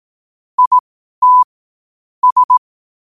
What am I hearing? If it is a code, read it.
Morse code, heard as a keyed tone, about 18 wpm, spaced as for 12 wpm, 1000 Hz. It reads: IT S
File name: qotdfiles / 12wpm